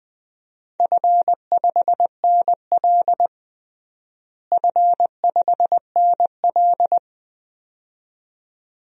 Morse code, heard as a keyed tone, about 20 wpm, 700 Hz. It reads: F5NL F5NL
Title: DXCC-100 Call Signs - 4 Characters - Repeated 2x - 20wpm